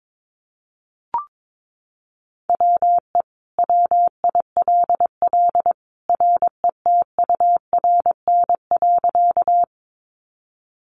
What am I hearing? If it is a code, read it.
Morse code, heard as a keyed tone, about 22 wpm, 700 Hz. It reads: WE WILL RETURN.
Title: Sentences from Top 800 Words 22wpm